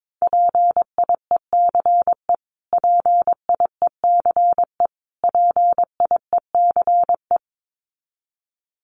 Morse code, heard as a keyed tone, about 22 wpm, 700 Hz. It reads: PIECE PIECE PIECE
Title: Top 500 Words Repeated 22wpm